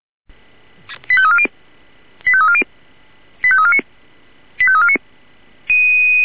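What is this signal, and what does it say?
Five-tone selective call (ZVEI), heard as quick runs of five tones, tones of about 70 ms. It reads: 85340 85340 85349 85349
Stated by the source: ZVEI five-tone call received with an SDR.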